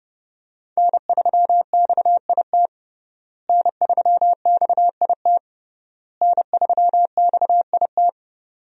Morse code, heard as a keyed tone, about 30 wpm, 700 Hz. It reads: N3XIT N3XIT N3XIT
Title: USA Call Signs - 5 Characters - Repeated 3x - 30wpm